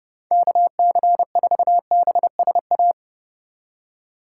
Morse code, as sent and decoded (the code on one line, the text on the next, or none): -.- -.-. ....- -... ... .-
KC4BSA